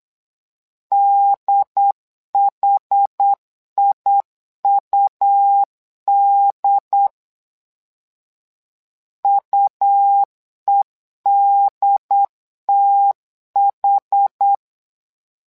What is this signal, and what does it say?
Morse code, heard as a keyed tone, about 8 wpm, 800 Hz. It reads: DHIUD UEDTH
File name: LEZIONI / 2 UHDN / 2 UHD 8wpm 2258